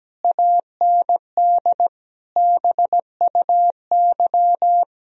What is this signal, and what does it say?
Morse code, heard as a keyed tone, about 17 wpm, 700 Hz. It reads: AND BUY
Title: Sentences from Top 1000 Words - Review Part 4 - 17wpm